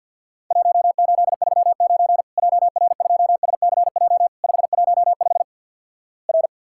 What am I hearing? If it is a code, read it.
Morse code, heard as a keyed tone, about 50 wpm, 700 Hz. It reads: 1839 JR2SCJ 515 R
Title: Worked All Europe DX Contest - QTC Exchange - 50wpm